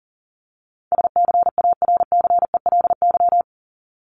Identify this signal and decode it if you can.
Morse code, heard as a keyed tone, about 40 wpm, 700 Hz. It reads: SCARCELY
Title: LY Suffix - Top 100 Words with Base Words 40wpm